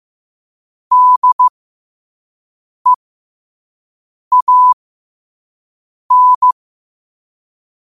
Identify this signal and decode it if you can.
Morse code, heard as a keyed tone, about 15 wpm, spaced as for 5 wpm, 1000 Hz. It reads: DEAN
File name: qotdfiles / 5wpm